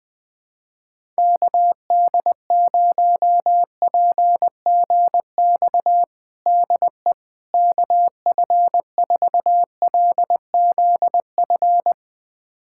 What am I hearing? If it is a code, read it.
Morse code, heard as a keyed tone, about 21 wpm, 700 Hz. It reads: KD0PGX DE KF4LZF
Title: USA Callsign DE Callsign 20wpm